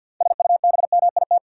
Morse code, heard as a keyed tone, about 51 wpm, 700 Hz. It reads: SUBMIT
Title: Six Letter Words - 50wpm